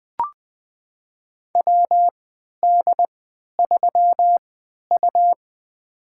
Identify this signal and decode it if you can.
Morse code, heard as a keyed tone, about 21 wpm, spaced as for 11 wpm, 700 Hz. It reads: WD3U
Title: USA Call Signs - 3x Character Spacing - 20wpm